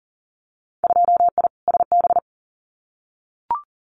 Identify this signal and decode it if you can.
Morse code, heard as a keyed tone, about 39 wpm, 700 Hz. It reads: 2I SB
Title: Winter Field Day Exchanges Encore 40wpm